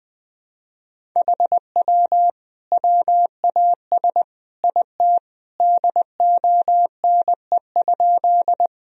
Morse code, heard as a keyed tone, about 20 wpm, 700 Hz. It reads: HW WAS IT DONE?
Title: Sentences from Top 500 Words and Abbreviations 20wpm